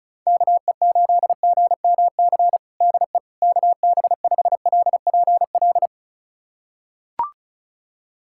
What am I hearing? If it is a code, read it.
Morse code, heard as a keyed tone, about 36 wpm, 700 Hz. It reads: KE8GMC DE KB5LPL
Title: USA Callsign DE Callsign - 6 Characters - 35wpm